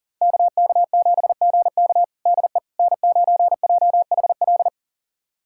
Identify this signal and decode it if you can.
Morse code, heard as a keyed tone, about 40 wpm, 700 Hz. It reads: KK7GK DE N9JHL
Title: USA Callsign DE Callsign - 5 Characters - 40wpm